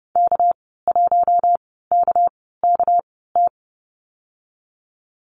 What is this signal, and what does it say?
Morse code, heard as a keyed tone, about 30 wpm, spaced as for 17 wpm, 700 Hz. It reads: K1KKT
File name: Lesson 37g - Letter x - Call Signs 30wpm